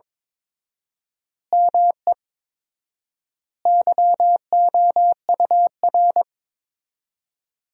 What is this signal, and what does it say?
Morse code, heard as a keyed tone, about 22 wpm, 700 Hz. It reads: ME YOUR
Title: 4x Word-Spacing - Sentences from Top 200 Words - 22wpm